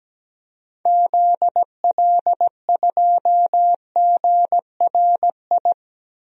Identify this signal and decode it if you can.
Morse code, heard as a keyed tone, about 17 wpm, 700 Hz. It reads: ZL2GRI
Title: DXCC-100 Call Signs - 6 Characters - 17wpm